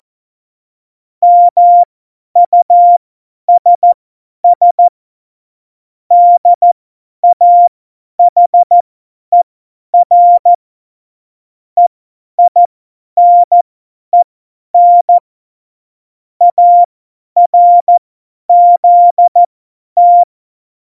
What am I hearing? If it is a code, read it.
Morse code, heard as a keyed tone, about 14 wpm, spaced as for 10 wpm, 700 Hz. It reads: MUSS DAHER EINEN ARZT
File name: Morsekurs / Witze / Witz 29. September 2020, 50 F70